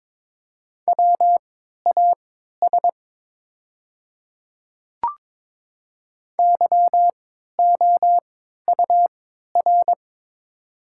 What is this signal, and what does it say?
Morse code, heard as a keyed tone, about 22 wpm, spaced as for 12 wpm, 700 Hz. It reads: WAS YOUR
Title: 3x Character Spacing - Top 300 Words - 22wpm